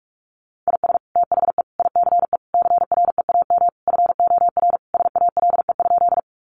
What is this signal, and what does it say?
Morse code, heard as a keyed tone, about 44 wpm, 700 Hz. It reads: IS THE ICE CREAM FOR SALE?
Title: Sentences from Top 1500 Words 45wpm